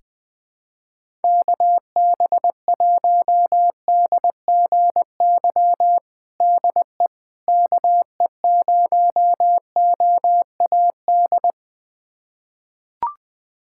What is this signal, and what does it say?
Morse code, heard as a keyed tone, about 20 wpm, 700 Hz. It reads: KB1DGY DE KE0OAD
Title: USA Callsign DE Callsign - 6 Characters - 20wpm